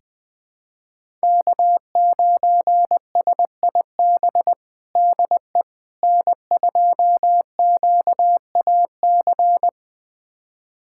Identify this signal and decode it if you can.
Morse code, heard as a keyed tone, about 20 wpm, 700 Hz. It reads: K9SIB DE N2QAC
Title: USA Callsign DE Callsign - 5 Characters - 20wpm